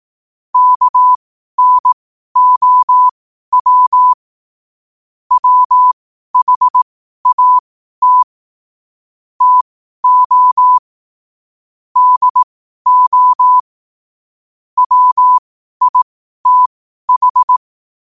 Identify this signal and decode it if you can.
Morse code, heard as a keyed tone, about 18 wpm, spaced as for 12 wpm, 1000 Hz. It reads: KNOW WHAT TO DO WITH
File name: qotdfiles / 12wpm